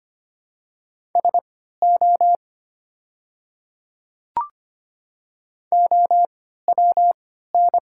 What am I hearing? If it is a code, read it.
Morse code, heard as a keyed tone, about 25 wpm, spaced as for 14 wpm, 700 Hz. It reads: SO OWN
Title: 3x Character Spacing - Top 300 Words - 25wpm